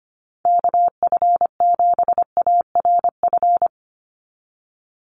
Morse code, heard as a keyed tone, about 25 wpm, 700 Hz. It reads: KF7ARF
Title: USA Call Signs 25wpm